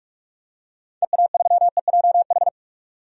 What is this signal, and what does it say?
Morse code, heard as a keyed tone, about 41 wpm, 700 Hz. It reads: EA3EJH
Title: DXCC-100 Call Signs - 6 Characters - 45wpm